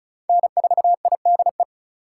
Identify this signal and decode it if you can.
Morse code, heard as a keyed tone, about 34 wpm, 700 Hz. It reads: N4IDE
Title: USA Call Signs - 8x Character Spacing - 35wpm